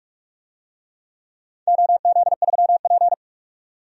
Morse code, heard as a keyed tone, about 46 wpm, 700 Hz. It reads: OZ3P
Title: DXCC-100 Call Signs - 4 Characters - 45wpm